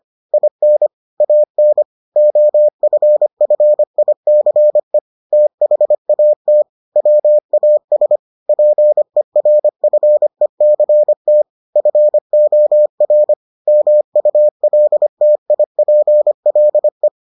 Morse code, heard as a keyed tone, about 25 wpm, 600 Hz. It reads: IN AN OFFICE THAT WAS PERFECT FOR MULTIPLE